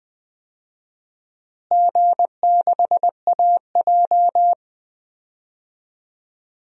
Morse code, heard as a keyed tone, about 20 wpm, 700 Hz. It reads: G6AJ
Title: DXCC-100 Call Signs - 4 Characters - Repeated 2x - 20wpm